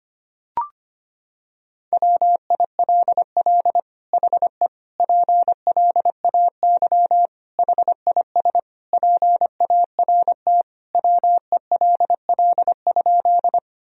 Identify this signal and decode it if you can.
Morse code, heard as a keyed tone, about 25 wpm, 700 Hz. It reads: WILL HE PLAY HIS PART WELL?
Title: Sentences from Top 1000 Words - Review Part 1 - 25wpm